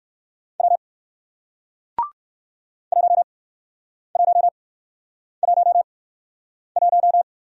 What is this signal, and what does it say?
Morse code, heard as a keyed tone, about 62 wpm, spaced as for 9 wpm, 700 Hz. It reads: W 1111